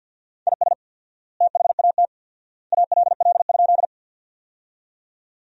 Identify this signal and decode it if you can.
Morse code, heard as a keyed tone, about 50 wpm, 700 Hz. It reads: IS THAT ALL?
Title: 4x Word-Spacing - Sentences from Top 100 Words - 50wpm